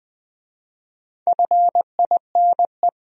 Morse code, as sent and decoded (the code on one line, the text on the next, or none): ..-. .. -. .
FINE